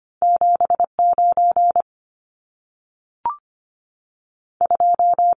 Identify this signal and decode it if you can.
Morse code, heard as a keyed tone, about 25 wpm, spaced as for 3 wpm, 700 Hz. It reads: 7 92
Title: Numbers - 2-Digit 25wpm